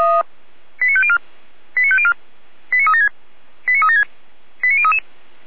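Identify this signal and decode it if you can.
Five-tone selective call (ZVEI), heard as quick runs of five tones, tones of about 70 ms. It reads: 89504 89504 89376 89376 89033
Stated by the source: ZVEI five-tone call received on SDR.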